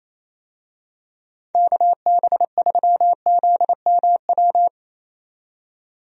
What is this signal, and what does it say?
Morse code, heard as a keyed tone, about 28 wpm, 700 Hz. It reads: KB3ZMW
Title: USA Call Signs - 6 Characters - 28wpm